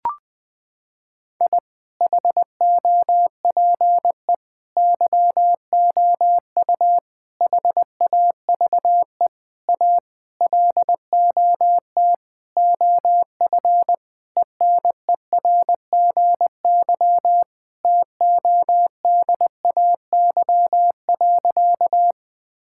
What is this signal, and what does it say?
Morse code, heard as a keyed tone, about 20 wpm, 700 Hz. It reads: I HOPE YOU HAVE A LOT OF ENERGY TODAY.
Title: Sentences from Top 1000 Words - Review Part 6 - 20wpm